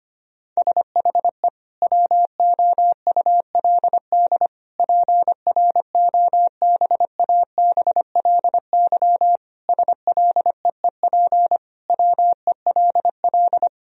Morse code, heard as a keyed tone, about 25 wpm, 700 Hz. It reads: SHE WOULD PROBABLY SLEEP WELL
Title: Sentences from Top 1000 Words - Review Part 4 - 25wpm